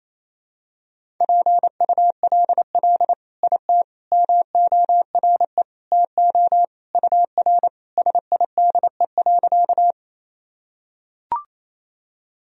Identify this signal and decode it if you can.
Morse code, heard as a keyed tone, about 28 wpm, 700 Hz. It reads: PULL IT MORE TO UR SIDE.
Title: Sentences from Top 400 Words and Abbreviations 28wpm